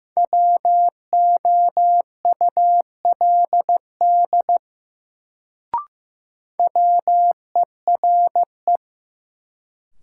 Morse code, heard as a keyed tone, about 16 wpm, 700 Hz. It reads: WOULD WERE
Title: Sentences from Top 100 Words - New Word Drill - 15wpm